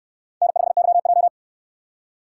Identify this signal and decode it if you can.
Morse code, heard as a keyed tone, about 68 wpm, 700 Hz. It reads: N5YJ